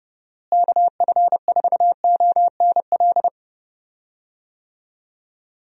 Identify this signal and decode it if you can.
Morse code, heard as a keyed tone, about 31 wpm, 700 Hz. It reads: KF4ONL
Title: USA Call Signs - 6 Characters - Repeated 2x - 30wpm